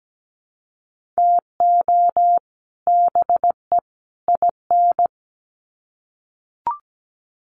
Morse code, heard as a keyed tone, about 17 wpm, 700 Hz. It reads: TO BE IN
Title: Sets of 3 Words 17wpm